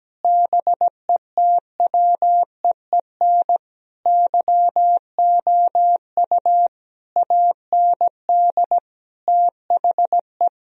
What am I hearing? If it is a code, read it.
Morse code, heard as a keyed tone, about 17 wpm, 700 Hz. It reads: BETWEEN YOU AND THE
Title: Sentences from Top 1000 Words - Review Part 4 - 17wpm